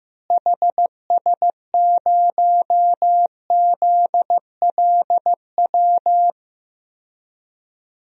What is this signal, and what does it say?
Morse code, heard as a keyed tone, about 15 wpm, 700 Hz. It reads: HS0ZLW
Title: DXCC-100 Call Signs - 6 Characters - Repeated 2x - 15wpm